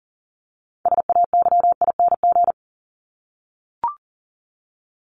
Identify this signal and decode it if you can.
Morse code, heard as a keyed tone, about 40 wpm, 700 Hz. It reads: SAYING
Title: ING Suffix - Top 100 Words with Base Words 40wpm